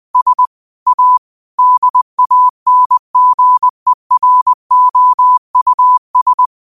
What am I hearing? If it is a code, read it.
Morse code, heard as a keyed tone, about 20 wpm, 1000 Hz. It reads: S A DANGEROUS
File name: qotdfiles / 20wpm